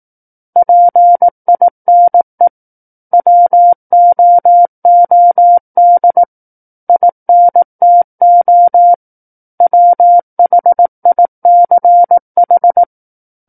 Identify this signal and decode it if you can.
Morse code, heard as a keyed tone, about 18 wpm, 700 Hz. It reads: PINE WOOD INTO WHICH
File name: 18 WPM